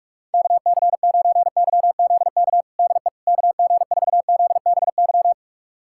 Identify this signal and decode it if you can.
Morse code, heard as a keyed tone, about 46 wpm, 700 Hz. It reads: KC9YZK DE KG4ZBY
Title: USA Callsign DE Callsign 45wpm